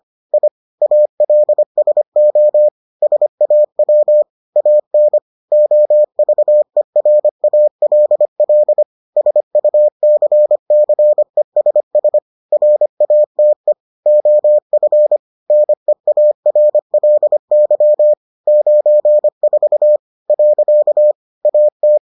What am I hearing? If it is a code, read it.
Morse code, heard as a keyed tone, about 25 wpm, 600 Hz. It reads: I ALSO SAW AN OVERALL SUCCESS RATE OF NEARLY 94 . AT